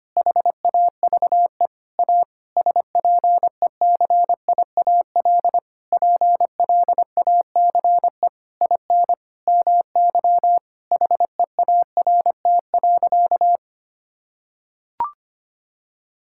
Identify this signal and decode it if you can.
Morse code, heard as a keyed tone, about 25 wpm, 700 Hz. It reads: HAVE A SPECIAL PLACE IN MY HEART.
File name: practice / 25wpm / Sentences with 9-Word Phrases - 25wpm